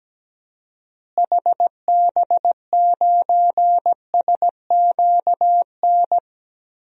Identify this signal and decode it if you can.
Morse code, heard as a keyed tone, about 17 wpm, 700 Hz. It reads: HB9SQN
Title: DXCC-100 Call Signs - 6 Characters - Repeated 2x - 17wpm